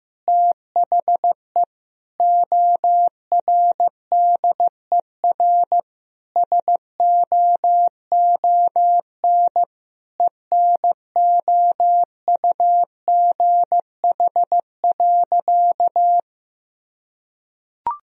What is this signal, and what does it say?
Morse code, heard as a keyed tone, about 15 wpm, 700 Hz. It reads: THE ORDER SOON ENOUGH.
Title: Sentences from Top 1000 Words - Review Part 2 - 15wpm